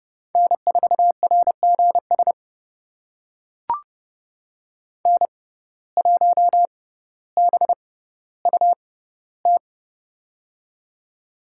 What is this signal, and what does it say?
Morse code, heard as a keyed tone, about 30 wpm, spaced as for 15 wpm, 700 Hz. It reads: N4RGS N1BUT
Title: USA Call Signs - 5 Characters - 6x Character Spacing - 30wpm